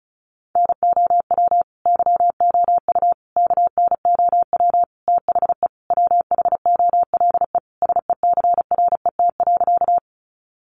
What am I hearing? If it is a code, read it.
Morse code, heard as a keyed tone, about 34 wpm, 700 Hz. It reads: NOW YOU KNOW THE WHOLE SECRET.
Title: Sentences from Top 1200 Words 35wpm